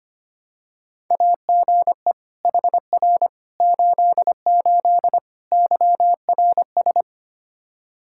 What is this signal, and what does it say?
Morse code, heard as a keyed tone, about 25 wpm, 700 Hz. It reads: AGE HR 88 YRS
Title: Practice QSO - Exchange 3 - More Info - 25wpm